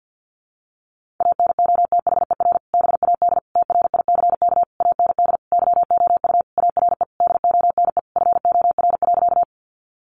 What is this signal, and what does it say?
Morse code, heard as a keyed tone, about 50 wpm, 700 Hz. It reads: ANOTHER BAD TRICK AND YOU ARE DONE FOR.
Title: Sentences from Top 1500 Words 50wpm